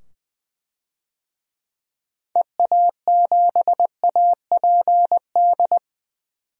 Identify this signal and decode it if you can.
Morse code, heard as a keyed tone, about 20 wpm, 700 Hz. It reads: EA7APD
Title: DXCC-100 Call Signs - 6 Characters - 20wpm